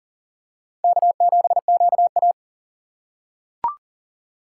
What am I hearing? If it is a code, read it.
Morse code, heard as a keyed tone, about 40 wpm, 700 Hz. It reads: K7QA
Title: USA Call Signs - 3x Character Spacing - 40wpm